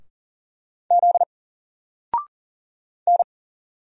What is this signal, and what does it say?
Morse code, heard as a keyed tone, about 40 wpm, 700 Hz. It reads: Z N
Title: Single Character (Letter:Number:Punctuation:Prosign) 40wpm